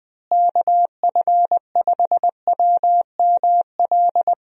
Morse code, heard as a keyed tone, about 20 wpm, 700 Hz. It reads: KF5WML
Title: USA Call Signs - 6 Characters - Repeated 3x - 20wpm